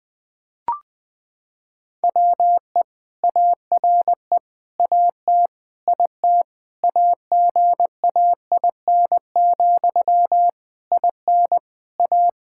Morse code, heard as a keyed tone, about 20 wpm, 700 Hz. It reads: WE ARE AT IT AGAIN, IN A
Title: Sentences from Top 1000 Words - Review Part 5 - 20wpm